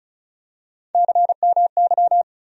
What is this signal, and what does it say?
Morse code, heard as a keyed tone, about 34 wpm, 700 Hz. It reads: CMY
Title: USA Call Sign Postfixes - 3 Letters 35wpm